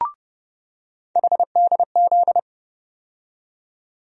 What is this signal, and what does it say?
Morse code, heard as a keyed tone, about 30 wpm, 700 Hz. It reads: HDZ
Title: USA Call Sign Postfixes - 3 Letters 30wpm